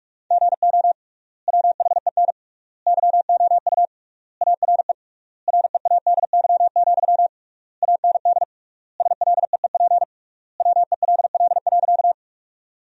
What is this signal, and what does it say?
Morse code, heard as a keyed tone, about 47 wpm, 700 Hz. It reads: GO WHEN YOU ARE READY, AND SLEEP WELL.
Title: 3x Word-Spacing - Sentences from Top 500 Words - 45wpm